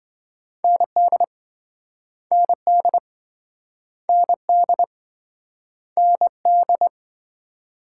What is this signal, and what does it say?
Morse code, heard as a keyed tone, about 26 wpm, 700 Hz. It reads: ND ND ND ND